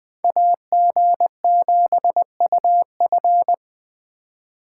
Morse code, heard as a keyed tone, about 20 wpm, 700 Hz. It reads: AG7UF
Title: Speed Racing - USA 2x2 Call Signs 17wpm